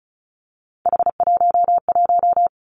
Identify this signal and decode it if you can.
Morse code, heard as a keyed tone, about 36 wpm, 700 Hz. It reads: H11
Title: Three Challenging Letter-Number Combinations 35wpm